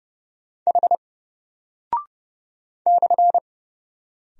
Morse code, heard as a keyed tone, about 30 wpm, 700 Hz. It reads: H /
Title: Single Character (Letter:Number:Punctuation:Prosign) Alt 30wpm